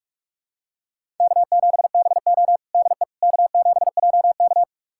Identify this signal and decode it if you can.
Morse code, heard as a keyed tone, about 46 wpm, 700 Hz. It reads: K7BO DE K7JK
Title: USA Callsign DE Callsign - 4 Characters - 45wpm